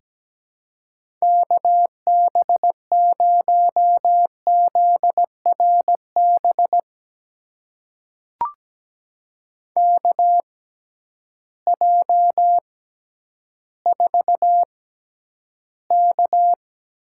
Morse code, heard as a keyed tone, about 17 wpm, spaced as for 10 wpm, 700 Hz. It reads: KB0ZRB KJ4K
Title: USA Call Signs - 6 Characters - 6x Character Spacing - 17wpm